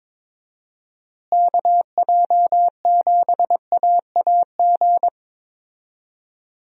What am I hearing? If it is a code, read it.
Morse code, heard as a keyed tone, about 22 wpm, 700 Hz. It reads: KJ7AAG
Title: USA Call Signs - 6 Characters - Repeated 2x - 22wpm